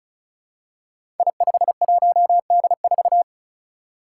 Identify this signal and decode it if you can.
Morse code, heard as a keyed tone, about 36 wpm, 700 Hz. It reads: I51D4
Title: Code Groups - 5 Letter:Numbers - 35wpm